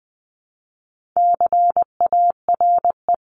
Morse code, heard as a keyed tone, about 20 wpm, 700 Hz. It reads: CARE
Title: Top 500 Words 20wpm